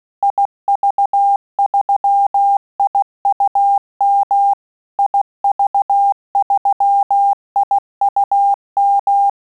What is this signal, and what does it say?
Morse code, heard as a keyed tone, about 16 wpm, 800 Hz. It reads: IV3IUM IV3IUM
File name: QSO CW 75 CPM